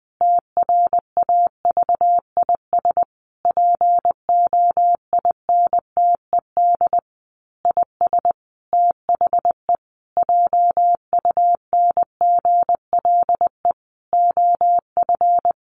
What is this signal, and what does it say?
Morse code, heard as a keyed tone, about 20 wpm, 700 Hz. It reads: TRAVIS POINTED IS THE JUNGLE OF
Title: A Sound Of Thunder 20wpm